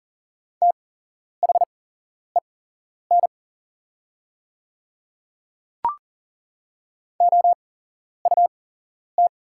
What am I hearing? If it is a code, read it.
Morse code, heard as a keyed tone, about 41 wpm, spaced as for 11 wpm, 690 Hz. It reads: THEN OUT
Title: 8x Character Spacing - Top 100 Words - 40wpm